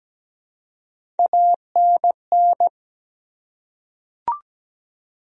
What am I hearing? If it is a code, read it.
Morse code, heard as a keyed tone, about 17 wpm, 700 Hz. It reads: ANN
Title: USA Common Names - Max 3 Letters - 17wpm